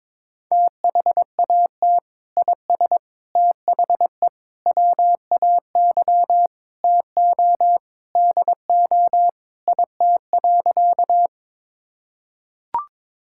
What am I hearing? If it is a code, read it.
Morse code, heard as a keyed tone, about 22 wpm, 700 Hz. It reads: THAT IS THE WAY TO DO IT.
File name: practice / 22wpm / Sentences with 3-Word Phrases - 22wpm